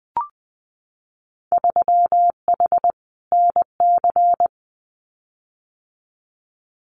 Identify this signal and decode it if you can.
Morse code, heard as a keyed tone, about 20 wpm, 700 Hz. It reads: 3H NC
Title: Winter Field Day Exchanges 20wpm